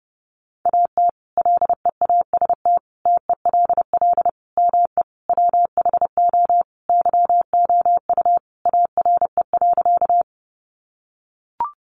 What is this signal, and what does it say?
Morse code, heard as a keyed tone, about 30 wpm, 700 Hz. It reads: AT LEAST TELL ME WHO YOU ARE.
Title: Sentences from Top 700 Words 30wpm